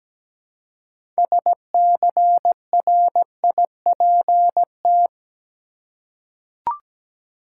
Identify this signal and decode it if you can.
Morse code, heard as a keyed tone, about 17 wpm, 700 Hz. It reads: SCRIPT
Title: Six Letter Words - 17wpm